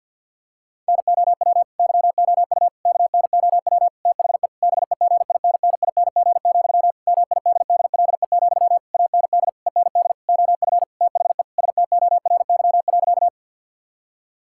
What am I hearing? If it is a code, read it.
Morse code, heard as a keyed tone, about 50 wpm, 700 Hz. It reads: NOW YOU KNOW THE BEGINNING, MIDDLE, AND END OF THE STORY.
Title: Sentences from Top 1000 Words - Review Part 6 - 50wpm